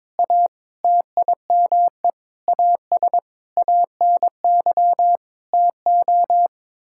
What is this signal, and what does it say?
Morse code, heard as a keyed tone, about 22 wpm, 700 Hz. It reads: A TIME AS ANY TO
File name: Sets of 9 Words - 22wpm